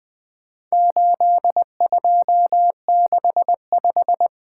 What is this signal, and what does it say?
Morse code, heard as a keyed tone, about 20 wpm, 700 Hz. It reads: 8265
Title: Four Letter-Number Combinations - 20wpm